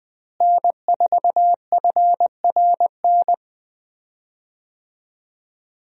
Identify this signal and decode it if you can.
Morse code, heard as a keyed tone, about 21 wpm, 700 Hz. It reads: N4FRN
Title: USA Call Signs - 5 Characters - Repeated 3x - 20wpm